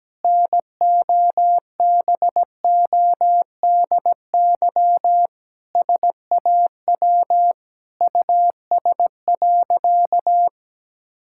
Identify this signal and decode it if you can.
Morse code, heard as a keyed tone, about 17 wpm, 700 Hz. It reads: NOBODY SAW US.
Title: Sentences from Top 1600 Words 17wpm